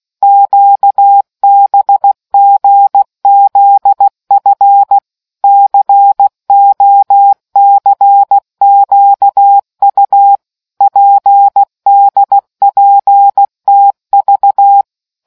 Morse code, heard as a keyed tone, about 16 wpm, 800 Hz. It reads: QBGZF COCQU PDPTV